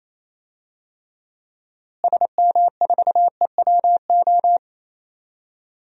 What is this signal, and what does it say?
Morse code, heard as a keyed tone, about 27 wpm, 700 Hz. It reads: SM4EWO
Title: DXCC-100 Call Signs - 6 Characters - Repeated 3x - 28wpm